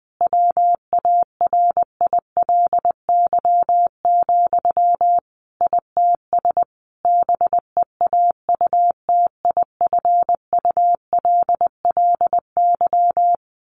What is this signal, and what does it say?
Morse code, heard as a keyed tone, about 20 wpm, 700 Hz. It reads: WARILY, ITS BEAUTIFULLY
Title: A Sound Of Thunder 20wpm